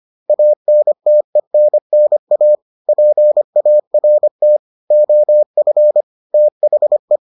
Morse code, heard as a keyed tone, about 25 wpm, 600 Hz. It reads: ANTENNA PART OF THE